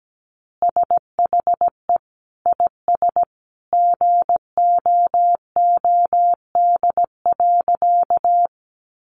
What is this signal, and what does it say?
Morse code, heard as a keyed tone, about 17 wpm, 700 Hz. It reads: SHE IS GOOD.
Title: Easy Phrases 17wpm